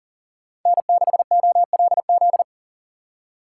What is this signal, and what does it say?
Morse code, heard as a keyed tone, about 50 wpm, 700 Hz. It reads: N6OLZ
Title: Speed Racing - USA 1x3 Call Signs 40wpm